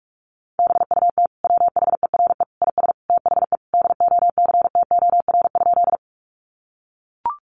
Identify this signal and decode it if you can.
Morse code, heard as a keyed tone, about 44 wpm, 700 Hz. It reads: BUT WHERE IS THE DOCTOR?
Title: Sentences from Top 1500 Words 45wpm